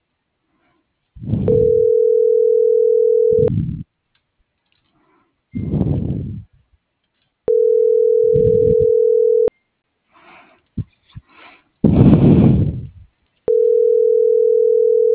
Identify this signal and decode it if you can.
Ringback tone (North America), 440 + 480 Hz.